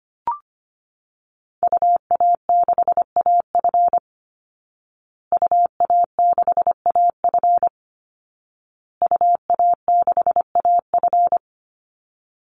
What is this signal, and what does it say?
Morse code, heard as a keyed tone, about 25 wpm, 700 Hz. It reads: UA6AF UA6AF UA6AF
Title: Call Signs Repeated 2x - 25wpm